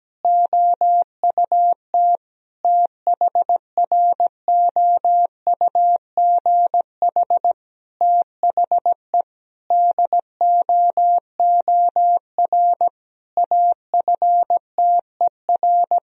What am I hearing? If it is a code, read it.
Morse code, heard as a keyed tone, about 17 wpm, 700 Hz. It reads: OUT THROUGH THE DOOR AFTER
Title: Sentences from Top 1000 Words - Review Part 1 - 17wpm